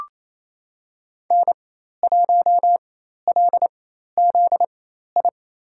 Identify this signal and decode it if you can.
Morse code, heard as a keyed tone, about 27 wpm, spaced as for 13 wpm, 700 Hz. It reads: N1LZI
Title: USA Call Signs - 5 Characters - 4x Character Spacing - 28wpm